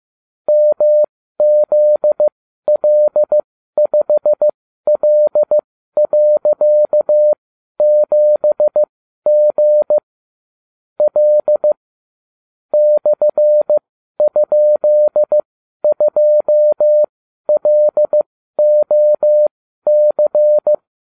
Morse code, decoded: MZL5L.7G L /?2LOC

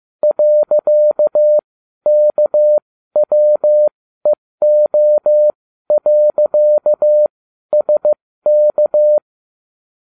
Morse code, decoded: .KWEO.SK

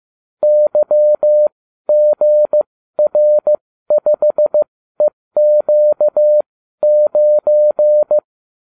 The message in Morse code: YGR5EQ9

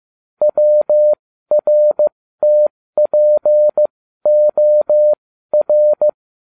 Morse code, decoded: WRTPOR

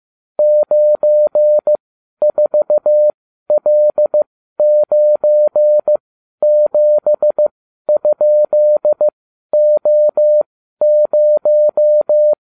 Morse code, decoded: 94L97?O0

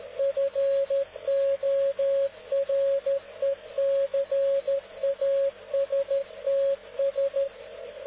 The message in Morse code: FORECASTS

